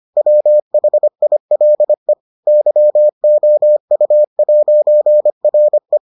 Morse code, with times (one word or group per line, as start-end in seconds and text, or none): 0.2-2.1 WHILE
2.5-6.0 YOU'RE